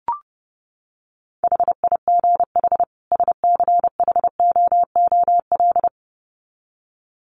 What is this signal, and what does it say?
Morse code, heard as a keyed tone, about 30 wpm, 700 Hz. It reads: HIGH SCHOOL
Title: Sets of 2 Words 30wpm